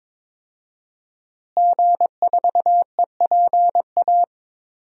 Morse code, heard as a keyed tone, about 21 wpm, 700 Hz. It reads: G4EPA
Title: DXCC-100 Call Signs - 5 Characters - Repeated 2x - 22wpm